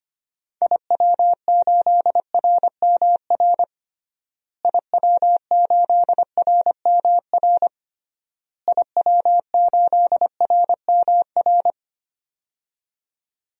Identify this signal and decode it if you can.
Morse code, heard as a keyed tone, about 25 wpm, 700 Hz. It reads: IW8RMR IW8RMR IW8RMR
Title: DXCC-100 Call Signs - 6 Characters - Repeated 3x - 25wpm